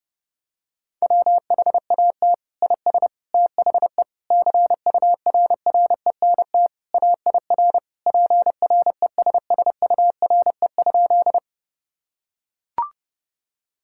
Morse code, heard as a keyed tone, about 30 wpm, 700 Hz. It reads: WHAT IS THE CURRENT AIR PRESSURE?
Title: Sentences from Top 1000 Words - Review Part 6 - 30wpm